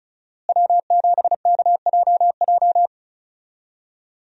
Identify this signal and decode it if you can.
Morse code, heard as a keyed tone, about 34 wpm, 700 Hz. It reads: W7KJJ
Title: USA Call Signs - 5 Characters - Repeated 2x - 35wpm